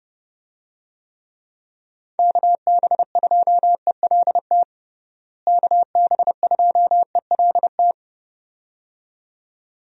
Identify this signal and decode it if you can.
Morse code, heard as a keyed tone, about 29 wpm, 700 Hz. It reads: KB2ELT KB2ELT
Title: USA Call Signs - 6 Characters - Repeated 2x - 30wpm